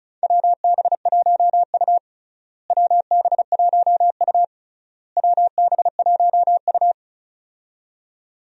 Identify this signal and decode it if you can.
Morse code, heard as a keyed tone, about 36 wpm, 700 Hz. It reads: WB1U WB1U WB1U
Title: USA Call Signs - 4 Characters - Repeated 3x - 35wpm